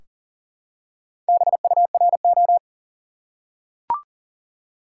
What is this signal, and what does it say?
Morse code, heard as a keyed tone, about 37 wpm, 700 Hz. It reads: BURO
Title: Top 180 QSO Elements 40wpm